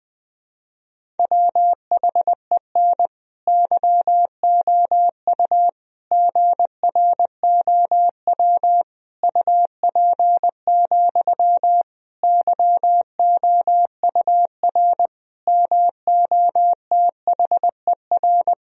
Morse code, heard as a keyed tone, about 20 wpm, 700 Hz. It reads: WHEN YOU GROW UP, YOUR MOTHER
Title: Sentences from Top 1000 Words - Review Part 1 - 20wpm